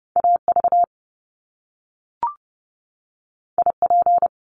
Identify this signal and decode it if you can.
Morse code, heard as a keyed tone, about 30 wpm, 700 Hz. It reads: AV IP
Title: Top Two Letter Combinations in Words 30wpm